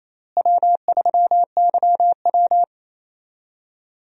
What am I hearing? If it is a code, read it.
Morse code, heard as a keyed tone, about 28 wpm, 700 Hz. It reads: W3YW
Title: USA Call Signs - 4 Characters - Repeated 3x - 28wpm